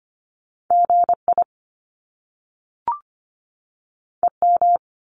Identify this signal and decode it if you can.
Morse code, heard as a keyed tone, about 25 wpm, 700 Hz. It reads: GI EM
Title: Top Two Letter Combinations in Words 25wpm